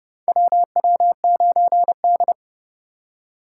Morse code, heard as a keyed tone, about 29 wpm, 700 Hz. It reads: WW9D
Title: USA Call Signs - 4 Characters - 6x Character Spacing - 30wpm